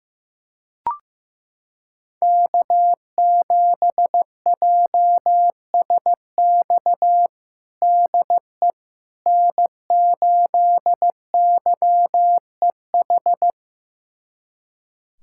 Morse code, heard as a keyed tone, about 15 wpm, 700 Hz. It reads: K7JSX DE N8YEH
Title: USA Callsign DE Callsign - 5 Characters - 15wpm